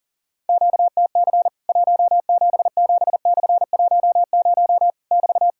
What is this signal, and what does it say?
Morse code, heard as a keyed tone, about 40 wpm, 700 Hz. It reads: QTC 177/10 <BT>
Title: Worked All Europe DX Contest - QTC Exchange - 40wpm